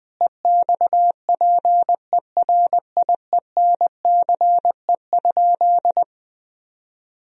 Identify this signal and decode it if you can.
Morse code, heard as a keyed tone, about 20 wpm, 700 Hz. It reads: EXPERIENCE?
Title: Sentences from Top 1000 Words - Review Part 3 - 20wpm